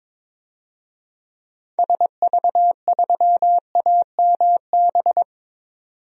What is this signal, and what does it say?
Morse code, heard as a keyed tone, about 21 wpm, 700 Hz. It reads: SV3AMB
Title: DXCC-100 Call Signs - 6 Characters - 22wpm